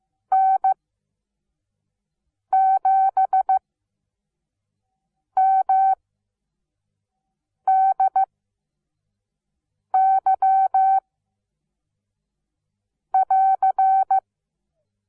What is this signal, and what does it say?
Morse code, heard as a keyed tone, about 15 wpm, spaced as for 4 wpm, 770 Hz. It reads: N7MDY<AR>
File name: Onlinelehrgang / morsen / DJ4UF Morsen17